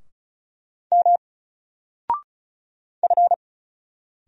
Morse code, heard as a keyed tone, about 36 wpm, 700 Hz. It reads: M F
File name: Single-Letter-Number-35wpm